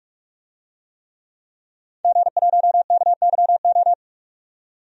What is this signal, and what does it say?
Morse code, heard as a keyed tone, about 46 wpm, 700 Hz. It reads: G1KYO
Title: DXCC-100 Call Signs - 5 Characters - 45wpm